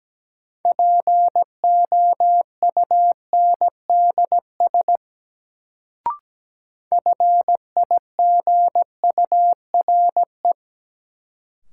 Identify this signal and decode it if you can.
Morse code, heard as a keyed tone, about 17 wpm, 700 Hz. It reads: POUNDS FIGURE
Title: Sentences from Top 1000 Words - Review Part 2 - New Word Drill - 17wpm